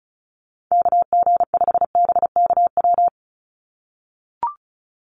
Morse code, heard as a keyed tone, about 34 wpm, 700 Hz. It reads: KG5BKW
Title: USA Call Signs 35wpm